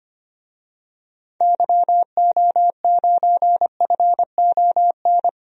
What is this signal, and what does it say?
Morse code, heard as a keyed tone, about 25 wpm, 700 Hz. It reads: YO9FON
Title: DXCC-100 Call Signs - 6 Characters - 25wpm